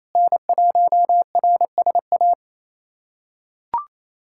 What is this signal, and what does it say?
Morse code, heard as a keyed tone, about 28 wpm, 700 Hz. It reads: N1RSA
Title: USA Callsign DE Callsign - 5 Characters - 28wpm